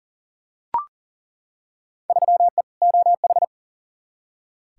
Morse code, heard as a keyed tone, about 37 wpm, 700 Hz. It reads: 3E OH
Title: Field Day Exchanges - 40wpm